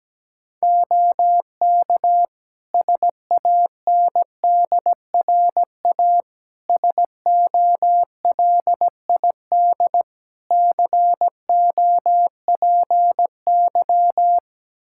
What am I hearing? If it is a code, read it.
Morse code, heard as a keyed tone, about 17 wpm, 700 Hz. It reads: OK SANDRA SOLID COPY
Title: Practice QSO - Exchange 3 - More Info - 17wpm